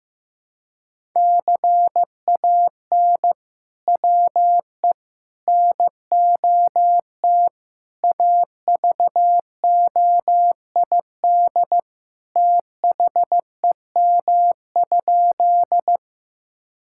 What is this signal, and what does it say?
Morse code, heard as a keyed tone, about 16 wpm, 700 Hz. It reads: CAN WE NOT AVOID THEM?
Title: Sentences from Top 1000 Words - Review Part 6 - 15wpm